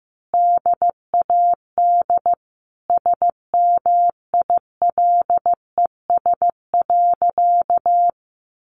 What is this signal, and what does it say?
Morse code, heard as a keyed tone, about 15 wpm, 700 Hz. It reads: DAD SMILES.
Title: Easy Phrases 15wpm